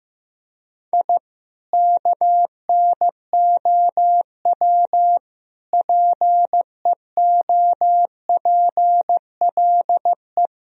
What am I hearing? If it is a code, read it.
Morse code, heard as a keyed tone, about 15 wpm, 700 Hz. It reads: I KNOW PEOPLE
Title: Sentences from Top 1000 Words - Review Part 4 - 15wpm